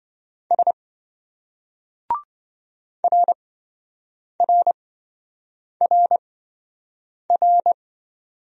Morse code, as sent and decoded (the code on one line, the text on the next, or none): ... / .-. .-. .-. .-.
S RRRR